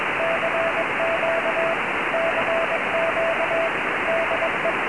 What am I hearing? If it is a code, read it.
Morse code, heard as a keyed tone, about 21 wpm, 660 Hz, with radio noise. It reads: CQ CQ DE
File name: ei5fk / rw1zc ik18pq cq